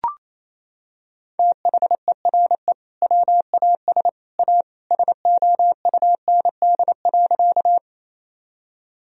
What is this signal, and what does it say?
Morse code, heard as a keyed tone, about 28 wpm, 700 Hz. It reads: THERE WAS A SOUND.
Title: Sentences from Top 1000 Words - Review Part 1 - 28wpm